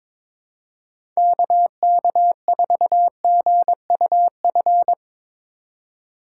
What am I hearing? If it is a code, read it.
Morse code, heard as a keyed tone, about 22 wpm, 700 Hz. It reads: KK4GUF
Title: USA Call Signs - 6 Characters - 3x Character Spacing - 22wpm